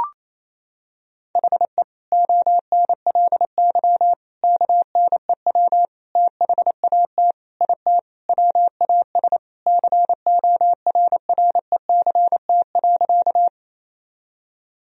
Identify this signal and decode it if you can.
Morse code, heard as a keyed tone, about 27 wpm, 700 Hz. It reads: HE ONLY KNEW THAT IT WAS CORRECT.
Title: Sentences from Top 1000 Words - Review Part 2 - 28wpm